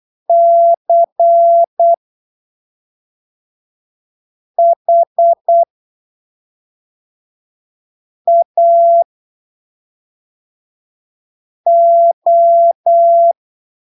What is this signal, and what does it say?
Morse code, heard as a keyed tone, about 8 wpm, spaced as for 3 wpm, 680 Hz. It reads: CHAO
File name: arquivos / texto / texto 21 08ppm